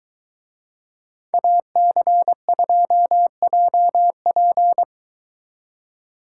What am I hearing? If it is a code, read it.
Morse code, heard as a keyed tone, about 22 wpm, 700 Hz. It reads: AC2JP